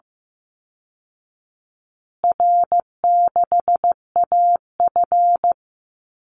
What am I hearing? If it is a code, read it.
Morse code, heard as a keyed tone, about 15 wpm, 700 Hz. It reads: R6AF
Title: Call Signs Repeated 2x - 15wpm